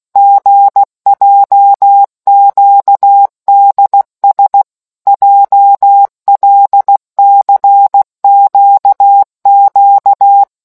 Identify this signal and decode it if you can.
Morse code, heard as a keyed tone, about 16 wpm, 800 Hz. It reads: GJQDS JLCQQ